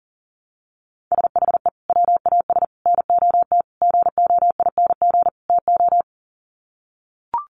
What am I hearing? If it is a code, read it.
Morse code, heard as a keyed tone, about 40 wpm, 700 Hz. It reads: SHE WAS NOT GOING TO
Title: Sets of 5 Words Encore2 40wpm